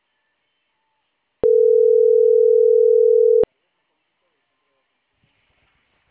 Ringback tone (North America), 440 + 480 Hz.